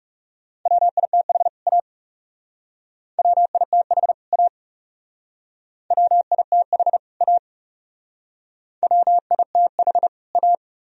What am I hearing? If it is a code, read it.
Morse code, heard as a keyed tone, about 35 wpm, 700 Hz. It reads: WITH A WITH A WITH A WITH A
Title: Speed-Racing - Sets of 2 Words 30wpm